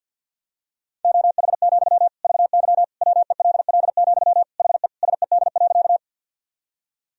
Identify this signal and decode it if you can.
Morse code, heard as a keyed tone, about 53 wpm, 700 Hz. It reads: OH, VY WELL, HE SED.
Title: Sentences from Top 500 Words and Abbreviations 50wpm